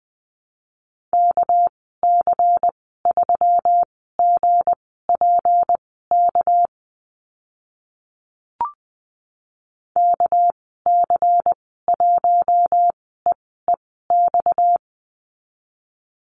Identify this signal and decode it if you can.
Morse code, heard as a keyed tone, about 20 wpm, spaced as for 14 wpm, 700 Hz. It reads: KC3GPK KC1EEX
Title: 245 Lesson 39g - Letter z - Call Signs 20wpm